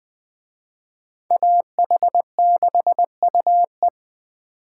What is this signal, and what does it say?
Morse code, heard as a keyed tone, about 21 wpm, 700 Hz. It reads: AH6UE
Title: Speed Racing - USA 2x2 Call Signs 17wpm